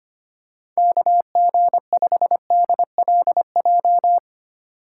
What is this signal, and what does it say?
Morse code, heard as a keyed tone, about 25 wpm, 700 Hz. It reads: KG5DLJ
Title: USA Call Signs - 6 Characters - 4x Character Spacing - 25wpm